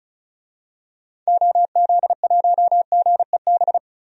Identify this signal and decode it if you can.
Morse code, heard as a keyed tone, about 36 wpm, 700 Hz. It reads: OZ1GEB
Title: DXCC-100 Call Signs - 6 Characters - 35wpm